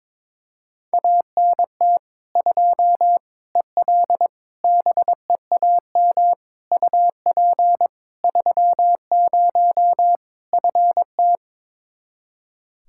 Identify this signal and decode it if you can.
Morse code, heard as a keyed tone, about 21 wpm, 700 Hz. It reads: ANT 2 EL BEAM UP 30 FT
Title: Practice QSO - Exchange 2 - Info - 22wpm